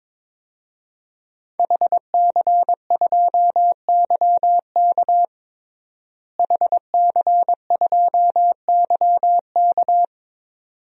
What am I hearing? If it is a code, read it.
Morse code, heard as a keyed tone, about 22 wpm, 700 Hz. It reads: HC2YK HC2YK
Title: DXCC-100 Call Signs - 5 Characters - Repeated 2x - 22wpm